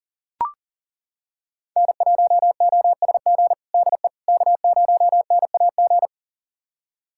Morse code, heard as a keyed tone, about 40 wpm, 700 Hz. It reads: N1OSG DE K0NAG
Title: USA Callsign DE Callsign - 5 Characters - 40wpm